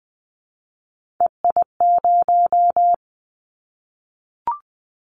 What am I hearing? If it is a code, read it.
Morse code, heard as a keyed tone, about 20 wpm, 700 Hz. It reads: EI0
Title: Callsign Prefixes for Top 50 DXCC - 20wpm